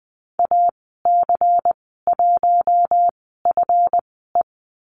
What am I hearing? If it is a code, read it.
Morse code, heard as a keyed tone, about 20 wpm, spaced as for 15 wpm, 700 Hz. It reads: AC1FE
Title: 154 Lesson 25g - Letter p - Call Signs 20wpm